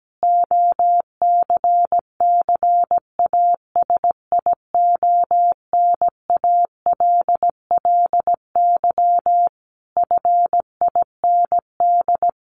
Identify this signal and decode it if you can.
Morse code, heard as a keyed tone, about 17 wpm, 700 Hz. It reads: OCCASIONALLY FIND
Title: Sentences from Top 1300 Words 17wpm